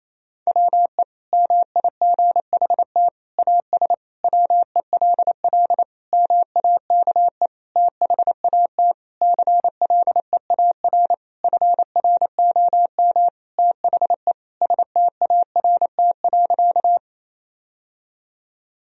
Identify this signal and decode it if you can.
Morse code, heard as a keyed tone, about 28 wpm, 700 Hz. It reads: WE MIGHT AS WELL MAKE THAT CLEAR FROM THE START.
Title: Sentences from Top 1000 Words - Review Part 1 - 28wpm